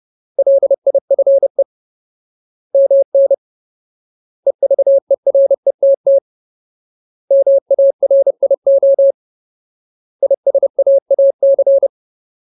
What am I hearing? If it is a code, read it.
Morse code, heard as a keyed tone, about 30 wpm, 550 Hz. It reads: LIFE MN EVERETT MARIO ISAAC
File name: Ultimate Word List / 30WPM Ultimate